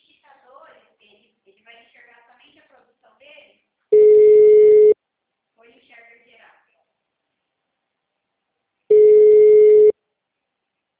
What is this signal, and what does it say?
Ringback tone (Europe, 425 Hz), with speech.